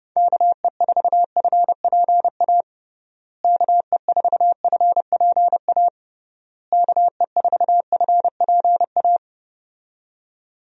Morse code, heard as a keyed tone, about 30 wpm, 700 Hz. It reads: KE4FPA KE4FPA KE4FPA
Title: USA Call Signs - 6 Characters - Repeated 3x - 30wpm